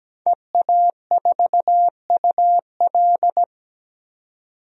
Morse code, heard as a keyed tone, about 17 wpm, 700 Hz. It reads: EA4UL
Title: DXCC-100 Call Signs - 5 Characters - 17wpm